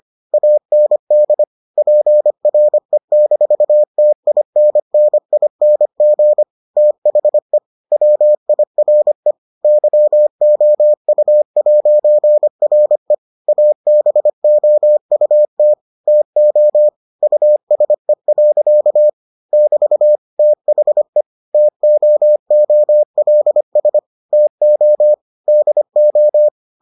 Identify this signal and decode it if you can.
Morse code, heard as a keyed tone, about 25 wpm, 600 Hz. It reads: AND PRE-TINNING THE WIRE YOU'RE ABOUT TO USE. <BT> THE TOOLS TO DO